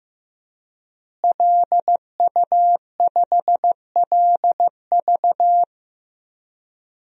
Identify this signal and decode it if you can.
Morse code, heard as a keyed tone, about 15 wpm, 700 Hz. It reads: LU5LV
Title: DXCC-100 Call Signs - 5 Characters - Repeated 3x - 15wpm